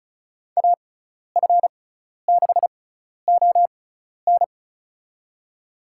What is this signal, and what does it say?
Morse code, heard as a keyed tone, about 36 wpm, spaced as for 12 wpm, 700 Hz. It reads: AF6ON
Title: USA Call Signs - 5 Characters - 6x Character Spacing - 35wpm